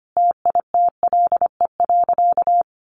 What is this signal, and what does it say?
Morse code, heard as a keyed tone, about 25 wpm, 700 Hz. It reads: TITLE.
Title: Sentences from Top 1200 Words 25wpm